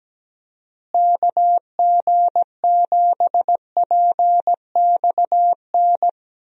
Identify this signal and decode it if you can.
Morse code, heard as a keyed tone, about 17 wpm, 700 Hz. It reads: KG7PXN
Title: USA Call Signs - 6 Characters - Repeated 2x - 17wpm